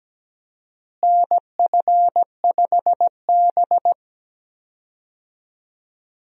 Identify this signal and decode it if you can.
Morse code, heard as a keyed tone, about 17 wpm, 700 Hz. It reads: NF5B
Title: USA Call Signs - 4 Characters - Repeated 2x - 17wpm